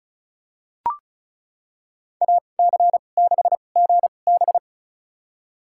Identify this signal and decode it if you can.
Morse code, heard as a keyed tone, about 34 wpm, spaced as for 25 wpm, 700 Hz. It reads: AC6GB
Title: USA Call Signs - 5 Characters - 2x Character Spacing - 35wpm